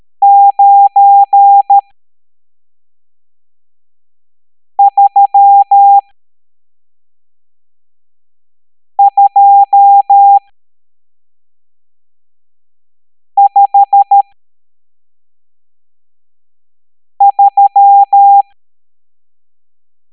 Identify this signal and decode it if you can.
Morse code, heard as a keyed tone, about 13 wpm, spaced as for 3 wpm, 800 Hz. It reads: 93253
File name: hamdata / L12-EX02-CHIFFRES